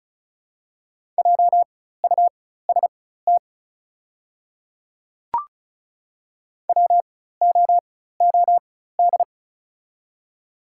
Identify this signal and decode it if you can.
Morse code, heard as a keyed tone, about 34 wpm, spaced as for 16 wpm, 700 Hz. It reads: JUST WOOD
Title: 4x Character Spacing - Top 300 Words - 35wpm